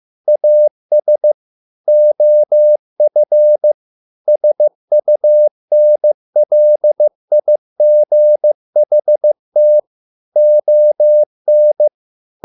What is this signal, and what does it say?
Morse code, heard as a keyed tone, about 15 wpm, 600 Hz. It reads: AS OF SUNLIGHT ON